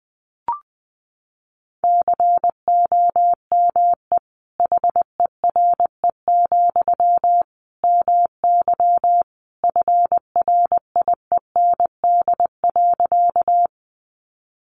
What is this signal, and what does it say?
Morse code, heard as a keyed tone, about 20 wpm, 700 Hz. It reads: COME HERE, MY FRIEND.
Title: Sentences from Top 300 Words 20wpm